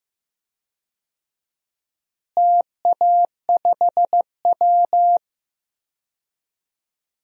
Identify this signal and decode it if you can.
Morse code, heard as a keyed tone, about 15 wpm, 700 Hz. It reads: TA5W
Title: DXCC-100 Call Signs - 4 Characters - Repeated 2x - 15wpm